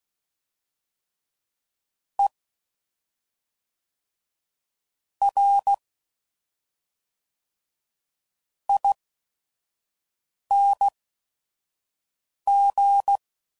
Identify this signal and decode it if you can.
Morse code, heard as a keyed tone, about 16 wpm, spaced as for 5 wpm, 800 Hz. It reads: E R ING
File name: Suffixes / Special ING's / SlowToFaster ING's